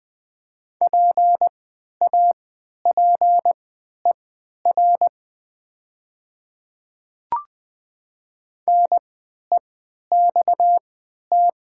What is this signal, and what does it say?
Morse code, heard as a keyed tone, about 20 wpm, spaced as for 11 wpm, 700 Hz. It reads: PAPER NEXT
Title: 3x Character Spacing - Top 300 Words - 20wpm